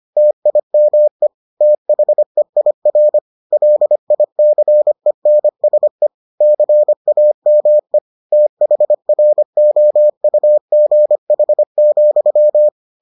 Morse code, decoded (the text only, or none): TIME THEIR LICENSE CAME THROUGH,